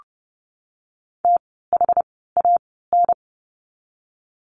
Morse code, heard as a keyed tone, about 30 wpm, spaced as for 17 wpm, 700 Hz. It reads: THAN
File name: course / 30:10wpm / 230 Lesson 37f - Letter x - Words 30wpm